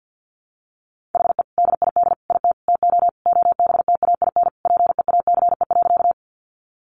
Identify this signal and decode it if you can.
Morse code, heard as a keyed tone, about 50 wpm, 700 Hz. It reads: HE DID IT TO OBTAIN PEACE.
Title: Sentences from Top 1500 Words 50wpm